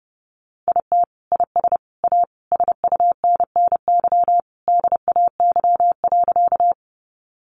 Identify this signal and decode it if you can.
Morse code, heard as a keyed tone, about 30 wpm, 700 Hz. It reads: IT IS A SUNNY DAY.
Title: Easy Phrases 30wpm